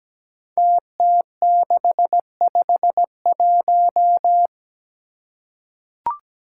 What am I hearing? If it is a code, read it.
Morse code, heard as a keyed tone, about 17 wpm, 700 Hz. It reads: TT651
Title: Five Challenging Letter-Number Combinations 17wpm